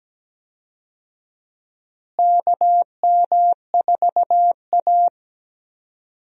Morse code, heard as a keyed tone, about 17 wpm, 700 Hz. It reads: KM4A
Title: USA Call Signs - 4 Characters - Repeated 2x - 17wpm